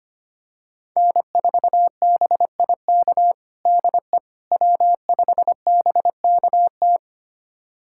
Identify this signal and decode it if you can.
Morse code, heard as a keyed tone, about 25 wpm, 700 Hz. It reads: N4BIK DE W5BKT
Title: USA Callsign DE Callsign - 5 Characters - 25wpm